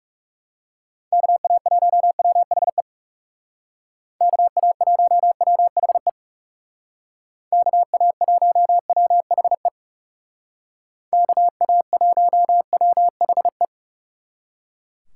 Morse code, heard as a keyed tone, about 36 wpm, 700 Hz. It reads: KA1WHE KA1WHE KA1WHE KA1WHE
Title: Speed Racing - USA 2x3 Call Signs 30wpm